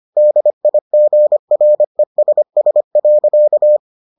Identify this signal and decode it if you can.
Morse code, heard as a keyed tone, about 25 wpm, 600 Hz. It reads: DIGRESS.